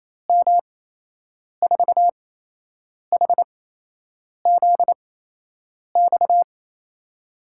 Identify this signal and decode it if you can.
Morse code, heard as a keyed tone, about 28 wpm, spaced as for 8 wpm, 700 Hz. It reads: M4HZX
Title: USA Call Signs - 6 Characters - 8x Character Spacing - 28wpm